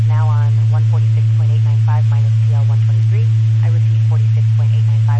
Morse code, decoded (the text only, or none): none